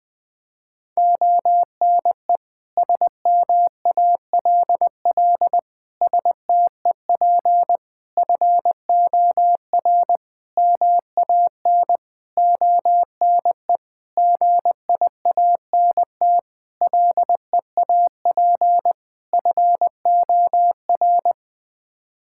Morse code, heard as a keyed tone, about 20 wpm, 700 Hz. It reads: ONE SMALL STEP FOR MAN ONE GIANT LEAP FOR
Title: Sets of 9 Words - 20wpm